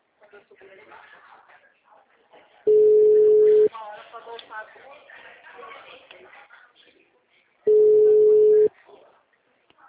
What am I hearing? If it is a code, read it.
Ringback tone (Europe, 425 Hz), with speech.
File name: out- -2002